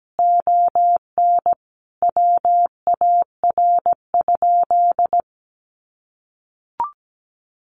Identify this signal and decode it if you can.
Morse code, heard as a keyed tone, about 17 wpm, 700 Hz. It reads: ON WAR?
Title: Sentences from Top 1200 Words 17wpm